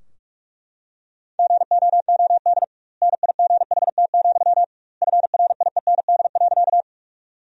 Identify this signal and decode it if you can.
Morse code, heard as a keyed tone, about 44 wpm, 700 Hz. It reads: GOOD NIGHT, FRIEND.
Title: 2x Word-Spacing - Sentences from Top 300 Words - 45wpm